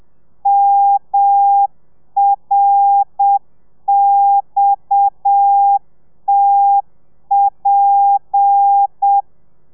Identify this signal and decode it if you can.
Morse code, heard as a keyed tone, about 7 wpm, 800 Hz. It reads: MRXTP
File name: telegrafia / i2tfj / Lez7 30